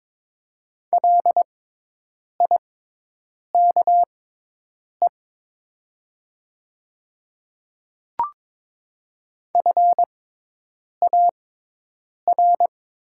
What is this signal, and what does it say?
Morse code, heard as a keyed tone, about 22 wpm, spaced as for 8 wpm, 700 Hz. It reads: LIKE FAR
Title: 6x Character Spacing - Top 300 Words - 22wpm